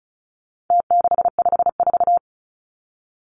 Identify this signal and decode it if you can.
Morse code, heard as a keyed tone, about 34 wpm, 700 Hz. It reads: T654
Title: Numbers - Contest Style - 4-Digit 35wpm